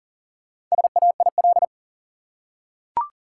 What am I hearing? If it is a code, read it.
Morse code, heard as a keyed tone, about 38 wpm, 700 Hz. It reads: SAIL
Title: Four Letter Words - 40wpm